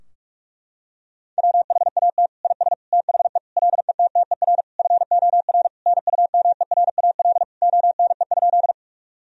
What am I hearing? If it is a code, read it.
Morse code, heard as a keyed tone, about 41 wpm, 700 Hz. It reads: WHAT IS THE LETTER FOR NUMERAL ONE?
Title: Sentences from Top 1000 Words - Review Part 2 - 45wpm